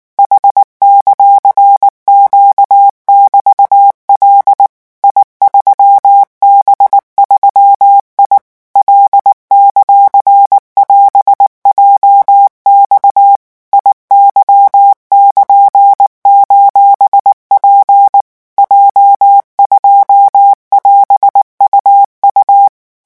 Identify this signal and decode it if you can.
Morse code, heard as a keyed tone, about 19 wpm, 800 Hz. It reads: H;Q<BT>L I3B3I L;<AS>JX IY<KN>:P J2<AS>UU
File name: morse mp3 / revision15-vitesse 18 mots